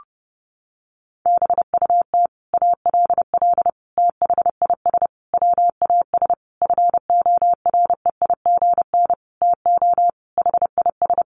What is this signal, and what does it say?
Morse code, heard as a keyed tone, about 30 wpm, 700 Hz. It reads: BUT ALL THIS WAS FOREIGN TO HIS
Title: Sentences from Top 1400 Words 30wpm